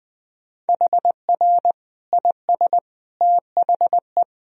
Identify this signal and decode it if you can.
Morse code, heard as a keyed tone, about 20 wpm, 700 Hz. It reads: HR IS THE
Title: Sentences from Top 300 Words and Abbreviations 20wpm